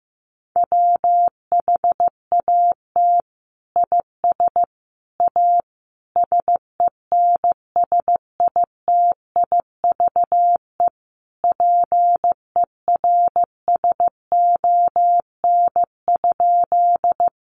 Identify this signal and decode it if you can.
Morse code, heard as a keyed tone, about 15 wpm, 700 Hz. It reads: WHAT IS A SENSITIVE PERSON?
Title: Sentences from Top 1400 Words 15wpm